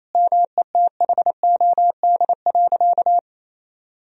Morse code, heard as a keyed tone, about 27 wpm, 700 Hz. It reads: METHOD.
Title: Sentences from Top 1000 Words - Review Part 3 - 28wpm